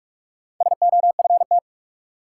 Morse code, heard as a keyed tone, about 46 wpm, 700 Hz. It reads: SOFT
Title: Sentences from Top 1500 Words - New Word Drill Alt - 45wpm